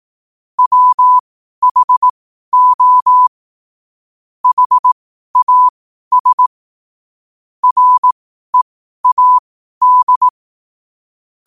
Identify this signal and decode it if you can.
Morse code, heard as a keyed tone, about 18 wpm, spaced as for 12 wpm, 1000 Hz. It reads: WHO HAS READ
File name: qotdfiles / 12wpm